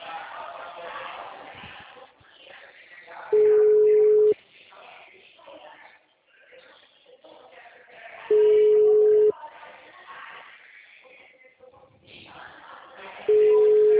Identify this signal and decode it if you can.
Ringback tone (Europe, 425 Hz), with speech.
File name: out- -2011- - - .147